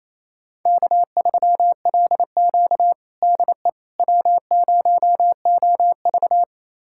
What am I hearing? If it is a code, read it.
Morse code, heard as a keyed tone, about 28 wpm, 700 Hz. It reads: K3LQ DE W0OV
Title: USA Callsign DE Callsign - 4 Characters - 28wpm